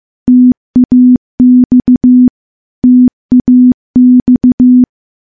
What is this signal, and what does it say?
Morse code, heard as a keyed tone, about 16 wpm, 250 Hz. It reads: TAX TAX
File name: morse / morse-training-15WPM